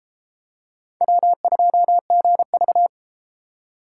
Morse code, heard as a keyed tone, about 28 wpm, 700 Hz. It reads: W2GV